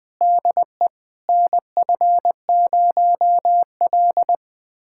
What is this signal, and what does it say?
Morse code, heard as a keyed tone, about 20 wpm, 700 Hz. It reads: DE NF0L
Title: USA Callsign DE Callsign - 4 Characters - 20wpm